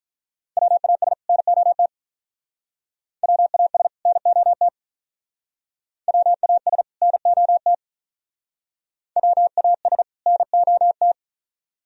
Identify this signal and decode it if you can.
Morse code, heard as a keyed tone, about 43 wpm, 700 Hz. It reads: WAS NOT WAS NOT WAS NOT WAS NOT
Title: Speed-Racing - Sets of 2 Words 35wpm